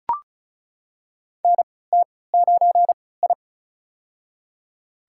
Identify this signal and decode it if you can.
Morse code, heard as a keyed tone, about 35 wpm, spaced as for 20 wpm, 700 Hz. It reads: NT9I
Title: USA Call Signs - 3x Character Spacing - 35wpm